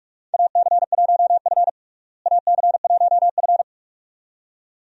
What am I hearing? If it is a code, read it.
Morse code, heard as a keyed tone, about 42 wpm, 700 Hz. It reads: AC1F AC1F
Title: USA Call Signs - 4 Characters - Repeated 2x - 45wpm